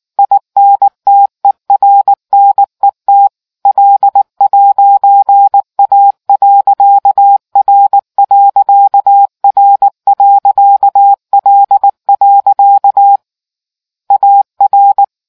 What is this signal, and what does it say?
Morse code, heard as a keyed tone, about 19 wpm, 800 Hz. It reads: INTERNET L'A.R.R.L. AR